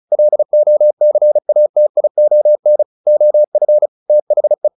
Morse code, heard as a keyed tone, about 34 wpm, 600 Hz. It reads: LOCATION OF THE
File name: the old man and the sea 2 35wpm